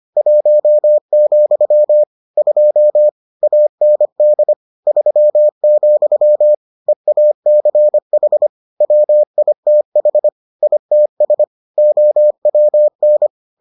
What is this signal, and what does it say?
Morse code, heard as a keyed tone, about 25 wpm, 600 Hz. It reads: 1, 2 AND 3, EACH WITH ITS OWN